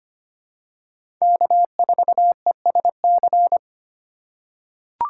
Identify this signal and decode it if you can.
Morse code, heard as a keyed tone, about 25 wpm, 700 Hz. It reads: K4ESC
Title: USA Call Signs - 5 Characters - 4x Character Spacing - 25wpm